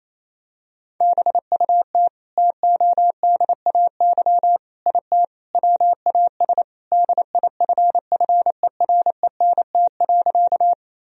Morse code, heard as a keyed tone, about 28 wpm, 700 Hz. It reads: BUT TODAY IT WAS DIFFERENT.
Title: Sentences from Top 1000 Words - Review Part 4 - 28wpm